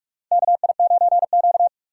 Morse code, heard as a keyed tone, about 46 wpm, 700 Hz. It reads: KI9Q
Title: USA Call Signs - 4 Characters - 6x Character Spacing - 45wpm